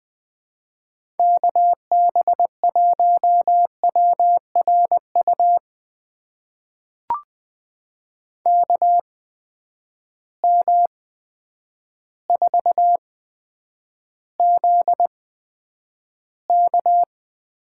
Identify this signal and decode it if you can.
Morse code, heard as a keyed tone, about 21 wpm, spaced as for 9 wpm, 700 Hz. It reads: KB1WRU KM4ZK
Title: USA Call Signs - 6 Characters - 8x Character Spacing - 20wpm